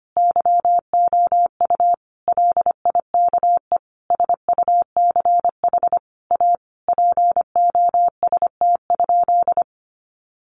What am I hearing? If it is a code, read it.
Morse code, heard as a keyed tone, about 25 wpm, 700 Hz. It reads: YOU LIKE SUCH A POST?
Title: Sentences from Top 1000 Words 25wpm